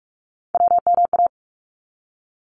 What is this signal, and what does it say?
Morse code, heard as a keyed tone, about 46 wpm, 700 Hz. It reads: WMA
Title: ARRL Sections 45wpm